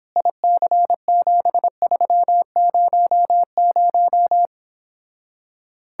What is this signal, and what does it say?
Morse code, heard as a keyed tone, about 31 wpm, 700 Hz. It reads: IC7300